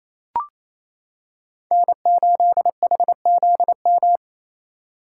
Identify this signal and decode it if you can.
Morse code, heard as a keyed tone, about 28 wpm, 700 Hz. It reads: N8HZM
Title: K1USN SST QSO Practice - 28wpm